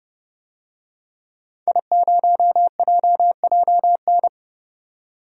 Morse code, heard as a keyed tone, about 32 wpm, 700 Hz. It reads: I0JJN
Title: DXCC-100 Call Signs - 5 Characters - 30wpm